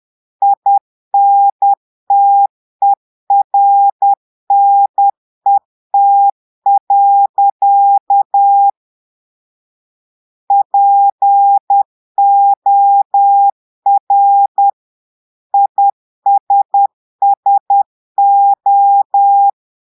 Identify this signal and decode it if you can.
Morse code, heard as a keyed tone, about 10 wpm, 800 Hz. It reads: INTERNET. POR ISSO
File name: portugais texte en clair 2 vitesse 10 mots minutes